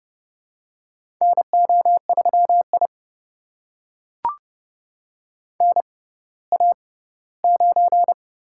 Morse code, heard as a keyed tone, about 30 wpm, spaced as for 17 wpm, 700 Hz. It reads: NO3I NA9
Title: USA Call Signs - 4 Characters - 6x Character Spacing - 30wpm